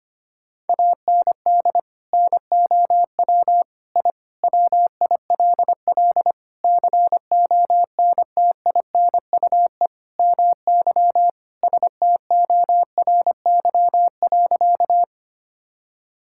Morse code, heard as a keyed tone, about 25 wpm, 700 Hz. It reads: AND NOW I WILL CONTINUE MY STORY.
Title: Sentences from Top 1000 Words - Review Part 6 - 25wpm